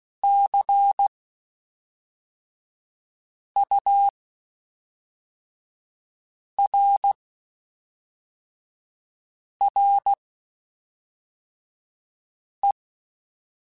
Morse code, heard as a keyed tone, about 16 wpm, 800 Hz. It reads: C U R R E
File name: Spaced Characters / Spaced Characters-mp3s / Weather5 22 sp.@5